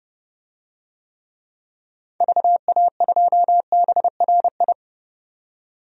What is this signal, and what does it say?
Morse code, heard as a keyed tone, about 30 wpm, 700 Hz. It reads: VA2BRI
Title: DXCC-100 Call Signs - 6 Characters - 30wpm